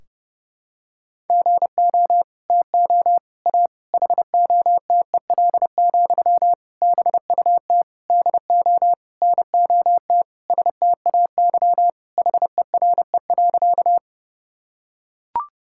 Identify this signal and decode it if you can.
Morse code, heard as a keyed tone, about 30 wpm, 700 Hz. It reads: GO TO A HOTEL, BUT DO NOT STAY HERE.
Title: Sentences from Top 1600 Words 30wpm